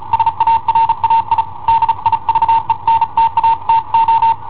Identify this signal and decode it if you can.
Morse code, heard as a keyed tone, about 35 wpm, 910 Hz. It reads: SARAI DIVENTATO
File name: i7ohp / cassetta i7ohp B